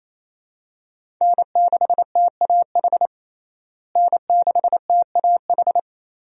Morse code, heard as a keyed tone, about 28 wpm, 700 Hz. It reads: N6TAH N6TAH
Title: USA Call Signs - 5 Characters - Repeated 2x - 28wpm